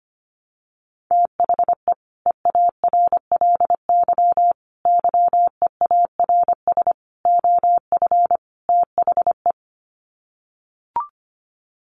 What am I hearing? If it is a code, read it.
Morse code, heard as a keyed tone, about 25 wpm, 700 Hz. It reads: THE EARLY YEARS OF THE
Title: Sets of 5 Words Encore 25wpm